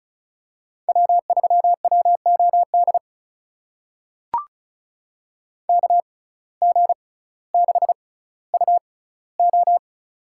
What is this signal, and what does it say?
Morse code, heard as a keyed tone, about 36 wpm, spaced as for 18 wpm, 700 Hz. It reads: W3WOD KG6UO
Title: USA Call Signs - 5 Characters - 6x Character Spacing - 35wpm